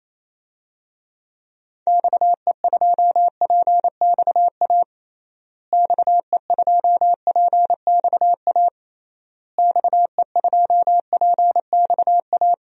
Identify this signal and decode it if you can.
Morse code, heard as a keyed tone, about 28 wpm, 700 Hz. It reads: XE2PXA XE2PXA XE2PXA
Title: DXCC-100 Call Signs - 6 Characters - Repeated 3x - 28wpm